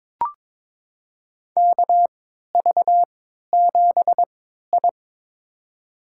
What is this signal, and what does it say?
Morse code, heard as a keyed tone, about 22 wpm, spaced as for 13 wpm, 700 Hz. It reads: KV7I
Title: USA Call Signs - 4 Characters - 3x Character Spacing - 22wpm